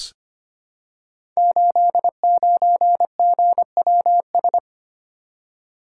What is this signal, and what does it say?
Morse code, heard as a keyed tone, about 25 wpm, 700 Hz. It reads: 89GWS
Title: Code Groups - 5 Letter:Numbers - 25wpm